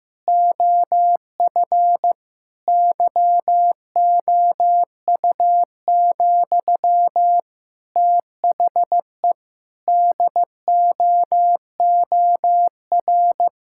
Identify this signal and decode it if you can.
Morse code, heard as a keyed tone, about 15 wpm, 700 Hz. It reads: OF YOU, THE DOOR
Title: Sentences from Top 1000 Words - Review Part 6 - 15wpm